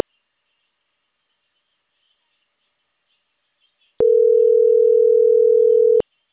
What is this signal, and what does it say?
Ringback tone (North America), 440 + 480 Hz.